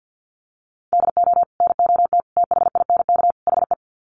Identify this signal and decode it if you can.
Morse code, heard as a keyed tone, about 50 wpm, 700 Hz. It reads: DO NOT THINK HE
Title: Sets of 4 Words 50wpm